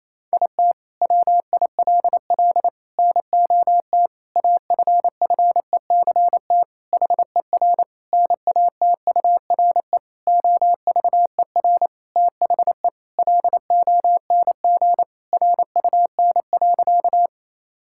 Morse code, heard as a keyed tone, about 28 wpm, 700 Hz. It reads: IT WILL NOT AFFECT HER NATURE OVER THE LONG RUN.
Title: Sentences from Top 1000 Words - Review Part 6 - 28wpm